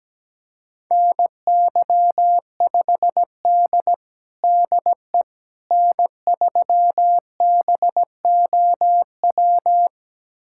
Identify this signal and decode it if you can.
Morse code, heard as a keyed tone, about 17 wpm, 700 Hz. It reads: NY5D DE N3BOW
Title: USA Callsign DE Callsign 17wpm